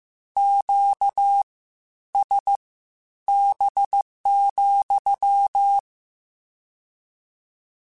Morse code, heard as a keyed tone, about 15 wpm, spaced as for 10 wpm, 800 Hz. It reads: QSB,
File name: Q Calls / Q Calls-mp3s / Q CALLS SPSL @13